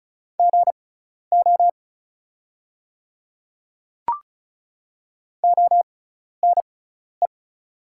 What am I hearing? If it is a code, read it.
Morse code, heard as a keyed tone, about 36 wpm, spaced as for 12 wpm, 700 Hz. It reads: GO ONE
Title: 6x Character Spacing - Top 100 Words - 35wpm